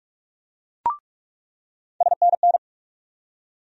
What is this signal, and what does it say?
Morse code, heard as a keyed tone, about 46 wpm, 700 Hz. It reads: SNN
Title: USA Call Sign Postfixes - 3 Letters 45wpm